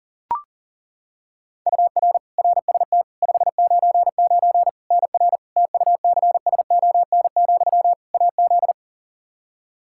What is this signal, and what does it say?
Morse code, heard as a keyed tone, about 40 wpm, 700 Hz. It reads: UR RST 599 NR TUCSON, AZ
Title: Practice QSO - Exchange 1 - Beginning - 40wpm